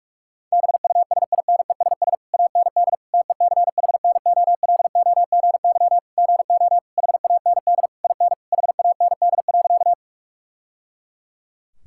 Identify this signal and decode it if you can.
Morse code, heard as a keyed tone, about 46 wpm, 700 Hz. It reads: BUSINESS AND TECHNOLOGY GO HAND IN HAND.
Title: Sentences from Top 1000 Words - Review Part 5 - 45wpm